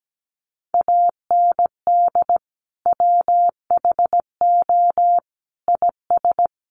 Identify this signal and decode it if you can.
Morse code, heard as a keyed tone, about 17 wpm, 700 Hz. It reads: AND WHO IS
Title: Sentences from Top 1100 Words 17wpm